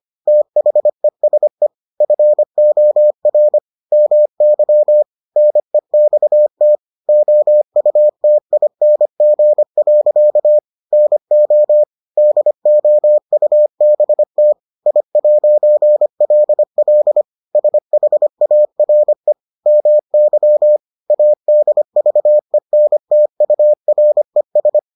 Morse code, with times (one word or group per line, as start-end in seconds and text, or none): 0.3-1.7 THESE
2.0-3.6 FOR
3.9-5.0 MY
5.4-6.8 NEXT
7.1-10.6 OUTING.
10.9-11.8 NO
12.2-14.5 DOUBT
14.9-17.2 I'LL
17.6-19.3 SHARE
19.7-20.8 MY
21.1-24.8 ADVENTURES